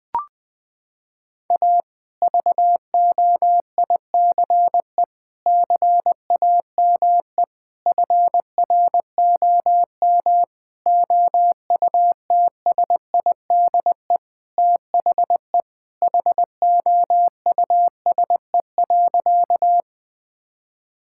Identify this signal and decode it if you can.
Morse code, heard as a keyed tone, about 20 wpm, 700 Hz. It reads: A VOICE CAME FROM OUTSIDE THE HOUSE.
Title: Sentences from Top 1000 Words - Review Part 6 - 20wpm